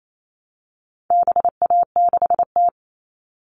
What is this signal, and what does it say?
Morse code, heard as a keyed tone, about 28 wpm, 700 Hz. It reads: BA6T
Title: All Cut Numbers - 4-Digit 28wpm